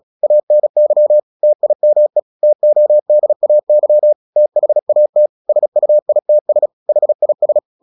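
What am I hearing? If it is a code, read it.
Morse code, heard as a keyed tone, about 37 wpm, 600 Hz. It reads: ANY TIME TODAY THAT SUITS HIS
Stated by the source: Morse practice at 35 wpm from CW med Gnister, sending Ken Follett's novel Never in English.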